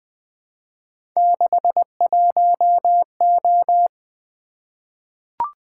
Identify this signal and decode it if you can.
Morse code, heard as a keyed tone, about 20 wpm, 700 Hz. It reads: 61O
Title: Three Letter-Number Combinations - 20wpm